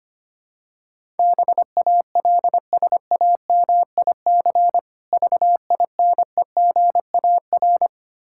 Morse code, heard as a keyed tone, about 25 wpm, 700 Hz. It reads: BALSAMIC VINEGAR
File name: practice / 25wpm / Difficult Words with Hints - 25wpm